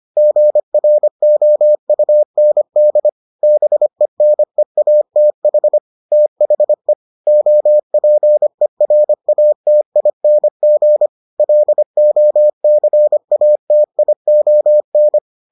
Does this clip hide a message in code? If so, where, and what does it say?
Yes: Morse code from 0.2 to 15.2 s, heard as a keyed tone, about 25 wpm, 600 Hz. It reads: GROUND BENEATH THE OPERATING LOCATION